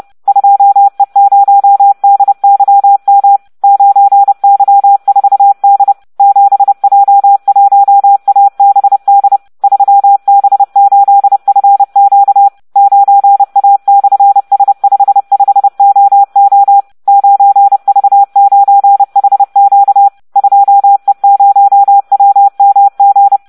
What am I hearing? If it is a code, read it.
Morse code, heard as a keyed tone, about 30 wpm, 800 Hz. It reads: 2E0DYM 9Y4D 7J1ABD 3B8FQ 9A/S55OO 9V9HQ 2E0WMG